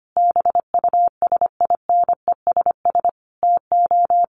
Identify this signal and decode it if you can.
Morse code, heard as a keyed tone, about 25 wpm, 700 Hz. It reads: BUSINESS TO
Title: Sentences from Top 1400 Words 25wpm